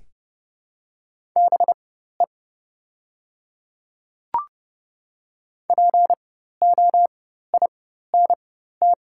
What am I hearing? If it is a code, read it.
Morse code, heard as a keyed tone, about 30 wpm, spaced as for 14 wpm, 700 Hz. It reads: BE POINT
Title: 4x Character Spacing - Top 200 Words - 30wpm